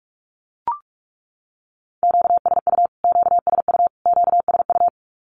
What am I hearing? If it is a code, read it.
Morse code, heard as a keyed tone, about 44 wpm, 700 Hz. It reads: QSU QSU QSU
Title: Q-Codes Repeated 3x 45wpm